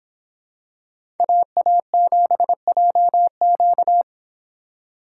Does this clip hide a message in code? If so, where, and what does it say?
Yes: Morse code from 1.2 to 4.0 s, heard as a keyed tone, about 25 wpm, 700 Hz. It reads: AA7JQ